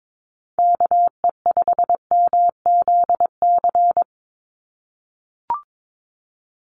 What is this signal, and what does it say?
Morse code, heard as a keyed tone, about 22 wpm, 700 Hz. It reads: KE5MZC
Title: USA Call Signs 22wpm